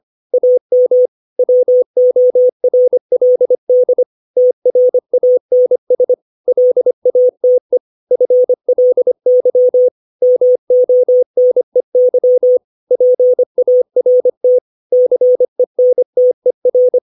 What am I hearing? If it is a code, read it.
Morse code, heard as a keyed tone, about 25 wpm, 500 Hz. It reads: AM WORLD TRANS LATE FLY MONEY PART CENTER
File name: 632 words / 632 Words 25WPM